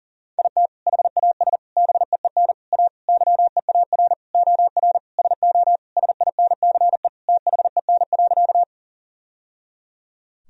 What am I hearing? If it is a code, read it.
Morse code, heard as a keyed tone, about 40 wpm, 700 Hz. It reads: IT HAS BEEN A YEAR OR SO SINCE THEN.
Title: Sentences from Top 1000 Words - Review Part 1 - 40wpm